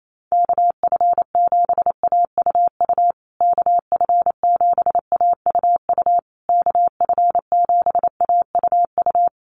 Morse code, heard as a keyed tone, about 28 wpm, 700 Hz. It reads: KF7AUU KF7AUU KF7AUU
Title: USA Call Signs Repeated 3x 28wpm